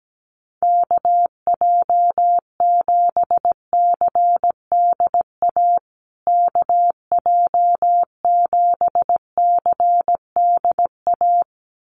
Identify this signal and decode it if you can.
Morse code, heard as a keyed tone, about 17 wpm, 700 Hz. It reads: KJ7CDA KJ7CDA
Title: USA Call Signs Repeated 2x 17wpm